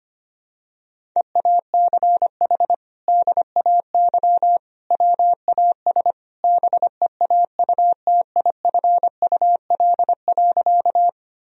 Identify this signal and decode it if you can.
Morse code, heard as a keyed tone, about 25 wpm, 700 Hz. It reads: EACH DAY WAS BEAUTIFUL.
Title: Sentences from Top 1000 Words - Review Part 1 - 25wpm